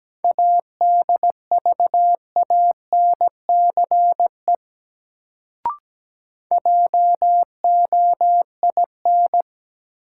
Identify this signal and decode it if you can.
Morse code, heard as a keyed tone, about 17 wpm, 700 Hz. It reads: ADVANCE JOIN
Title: Sentences from Top 1400 Words - New Word Drill Alt - 17wpm